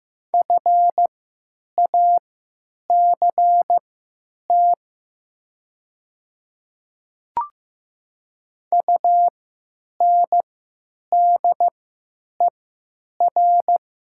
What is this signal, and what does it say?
Morse code, heard as a keyed tone, about 15 wpm, spaced as for 9 wpm, 700 Hz. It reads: FACT UNDER
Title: 3x Character Spacing - Top 500 Words - 15wpm